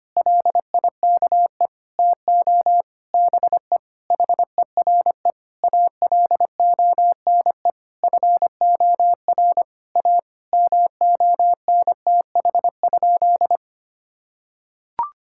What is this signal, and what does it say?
Morse code, heard as a keyed tone, about 25 wpm, 700 Hz. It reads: LIKE TO BE HERE ALONE FOR A MONTH?
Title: Sentences from Top 1000 Words - Review Part 5 - 25wpm